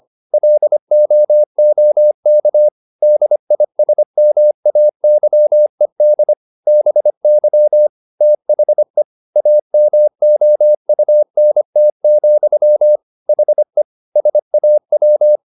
Morse code, heard as a keyed tone, about 25 wpm, 600 Hz. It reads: LOOK DISMAYED BY THE AMOUNT, HE SAW